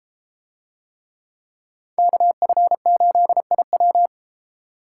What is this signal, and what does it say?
Morse code, heard as a keyed tone, about 32 wpm, 700 Hz. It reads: KF8IW